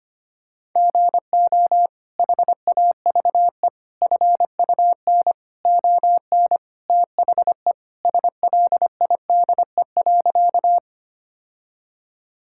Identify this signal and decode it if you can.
Morse code, heard as a keyed tone, about 25 wpm, 700 Hz. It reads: GO HAVE FUN ON THE SLIDE.
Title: Sentences from Top 1600 Words 25wpm